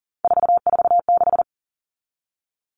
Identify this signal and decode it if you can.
Morse code, heard as a keyed tone, about 40 wpm, 700 Hz. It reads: 446
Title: Numbers - 3-Digit 40wpm